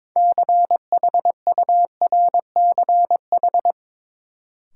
Morse code, heard as a keyed tone, about 21 wpm, 700 Hz. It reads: CHURCH
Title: Sets of 5 Words - New Word Drill - 22wpm